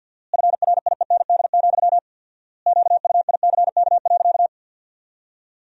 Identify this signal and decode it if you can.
Morse code, heard as a keyed tone, about 50 wpm, 700 Hz. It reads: FRIEND, QUICK.
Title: 4x Word-Spacing - Sentences from Top 500 Words - 50wpm